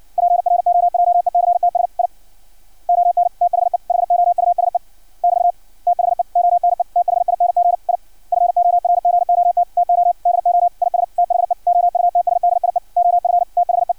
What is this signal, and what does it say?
Morse code, heard as a keyed tone, about 56 wpm, 700 Hz. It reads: PROJECTS I ON THE HOUSE <BT> THE ONE THING I FORGOT TO DO IS THE OUTSDIE OF THE
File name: archived / CRSnet